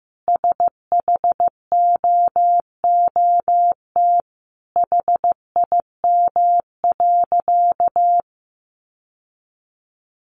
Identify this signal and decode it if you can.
Morse code, heard as a keyed tone, about 15 wpm, 700 Hz. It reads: SHOOT HIM.
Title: Sentences from Top 1200 Words 15wpm